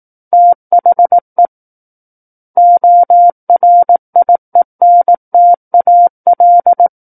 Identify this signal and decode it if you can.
Morse code, heard as a keyed tone, about 18 wpm, 700 Hz. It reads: THE ORIENTAL
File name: 18 WPM